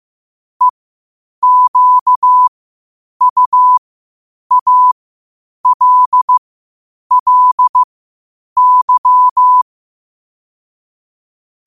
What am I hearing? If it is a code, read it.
Morse code, heard as a keyed tone, about 15 wpm, spaced as for 8 wpm, 1000 Hz. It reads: EQUALLY
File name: qotdfiles / 8wpm